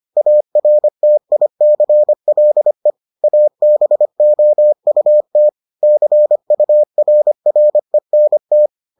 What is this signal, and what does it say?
Morse code, heard as a keyed tone, about 25 wpm, 600 Hz. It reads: ARTICLE ABOUT CURRENT